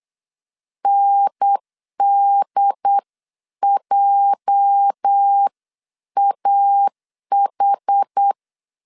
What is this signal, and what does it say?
Morse code, heard as a keyed tone, about 8 wpm, 800 Hz. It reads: NDJAH
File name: CW / DET / 1 9 1500C S C 8wpm 5653